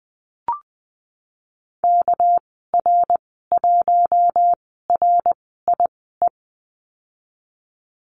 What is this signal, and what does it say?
Morse code, heard as a keyed tone, about 20 wpm, spaced as for 15 wpm, 700 Hz. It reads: KR1RIE
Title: 231 Lesson 37g - Letter x - Call Signs 20wpm